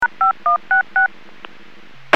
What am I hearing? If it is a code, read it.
DTMF, heard as telephone touch tones, tones of about 130 ms. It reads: #5166